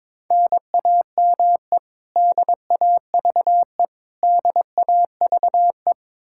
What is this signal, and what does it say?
Morse code, heard as a keyed tone, about 22 wpm, 700 Hz. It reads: NAME DAVE DAVE
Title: Practice QSO - Exchange 1 - Beginning - 22wpm